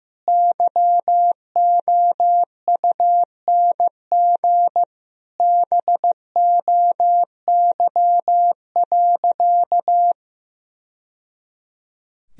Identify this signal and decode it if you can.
Morse code, heard as a keyed tone, about 15 wpm, 700 Hz. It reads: YOUNG BOY.
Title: Sentences from Top 1000 Words - Review Part 4 - 15wpm